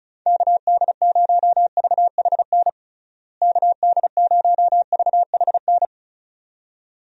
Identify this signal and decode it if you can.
Morse code, heard as a keyed tone, about 36 wpm, 700 Hz. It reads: KD0VHN KD0VHN
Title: USA Call Signs - 6 Characters - Repeated 2x - 35wpm